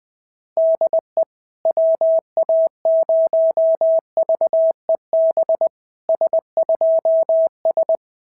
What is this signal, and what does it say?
Morse code, heard as a keyed tone, about 18 wpm, 650 Hz. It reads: DE WA0VEB S2S
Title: SOTA QSO Practice 20wpm